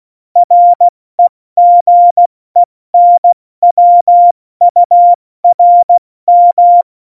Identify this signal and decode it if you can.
Morse code, heard as a keyed tone, about 16 wpm, 700 Hz. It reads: REGENWURM
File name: Morsekurs / Witze / Witz 14. August 2018, 70 F80